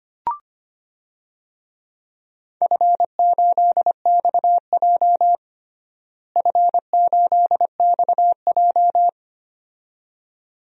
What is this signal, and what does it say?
Morse code, heard as a keyed tone, about 25 wpm, 700 Hz. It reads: F8XJ F8XJ
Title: DXCC-100 Call Signs - 4 Characters - Repeated 2x - 25wpm